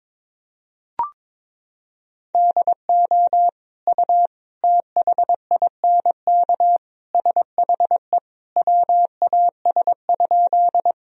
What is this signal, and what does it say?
Morse code, heard as a keyed tone, about 22 wpm, 700 Hz. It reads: DO U THINK SHE WAS?
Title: Sentences from Top 200 Words and Abbreviations 22wpm